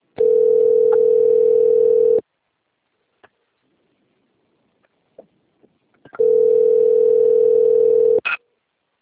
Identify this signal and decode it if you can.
Ringback tone (North America), 440 + 480 Hz.
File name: rg-1600-unknown- - - .10643